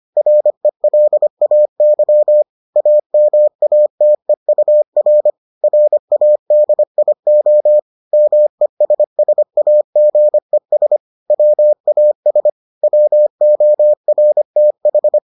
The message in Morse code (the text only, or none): RELAY AMATEUR RADIO MESSAGES WAS WORTH